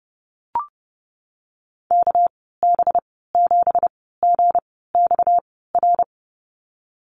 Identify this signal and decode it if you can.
Morse code, heard as a keyed tone, about 30 wpm, spaced as for 17 wpm, 700 Hz. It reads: KB7GXR
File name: Lesson 37g - Letter x - Call Signs 30wpm